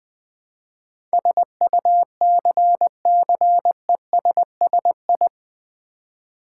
Morse code, heard as a keyed tone, about 20 wpm, 700 Hz. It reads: SUCCESSI
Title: Build A Word - 4 Letters At A Time - 10-Letter Words - 20wpm